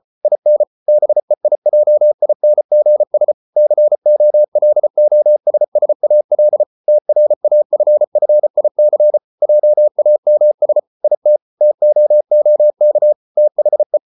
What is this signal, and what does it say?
Morse code, heard as a keyed tone, about 34 wpm, 600 Hz. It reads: IN BEIJINGS COLOSSAL TRAFFIC JAMS IT TOOK THE